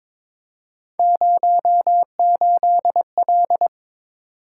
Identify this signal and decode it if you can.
Morse code, heard as a keyed tone, about 23 wpm, 700 Hz. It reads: 08L
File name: practice / 22wpm / Three Letter-Number Combinations - 22wpm